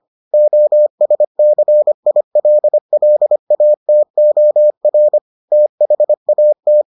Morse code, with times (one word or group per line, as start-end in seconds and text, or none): 0.3-5.2 OSCILLATOR
5.5-6.8 THAT